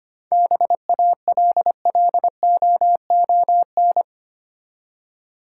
Morse code, heard as a keyed tone, about 25 wpm, 700 Hz. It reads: BALLOON
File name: practice / 25wpm / Easy Words with Hints - 25wpm